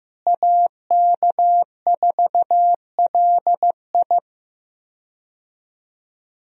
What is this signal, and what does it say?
Morse code, heard as a keyed tone, about 15 wpm, 700 Hz. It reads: AK4LI
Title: USA Call Signs - 5 Characters - Repeated 3x - 15wpm